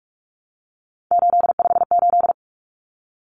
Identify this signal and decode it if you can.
Morse code, heard as a keyed tone, about 44 wpm, 700 Hz. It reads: 858
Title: Numbers - 3-Digit 45wpm